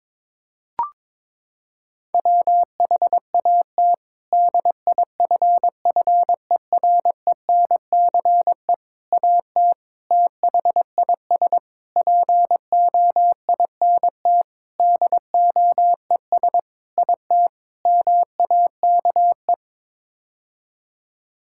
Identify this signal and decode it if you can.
Morse code, heard as a keyed tone, about 22 wpm, 700 Hz. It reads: WHAT DIFFERENCE AT THIS POINT DOES IT MAKE
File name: practice / 22wpm / Sets of 8 Words - 22wpm